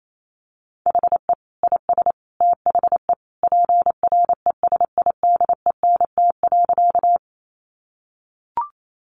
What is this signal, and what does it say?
Morse code, heard as a keyed tone, about 28 wpm, 700 Hz. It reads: HE IS THE PRESIDENT.
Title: Sentences from Top 1000 Words 28wpm